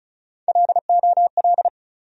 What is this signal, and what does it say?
Morse code, heard as a keyed tone, about 36 wpm, 700 Hz. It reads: LOL
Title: USA Call Sign Postfixes - 3 Letters 35wpm